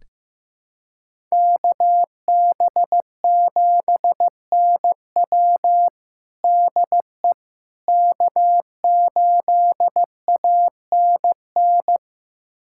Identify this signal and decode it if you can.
Morse code, heard as a keyed tone, about 15 wpm, 700 Hz. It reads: KB7NW DE K8ANN
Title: USA Callsign DE Callsign - 5 Characters - 15wpm